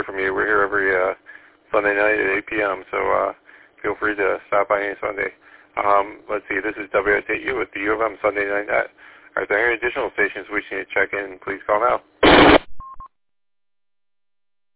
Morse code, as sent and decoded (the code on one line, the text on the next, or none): none